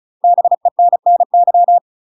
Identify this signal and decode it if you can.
Morse code, heard as a keyed tone, about 34 wpm, 700 Hz. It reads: BENNY
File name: UK MASTER QSOs TRAINING / 35WPM / UK MASTER QSOs 0015